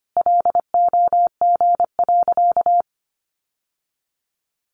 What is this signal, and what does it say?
Morse code, heard as a keyed tone, about 25 wpm, 700 Hz. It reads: LOG.
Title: Sentences from Top 1500 Words 25wpm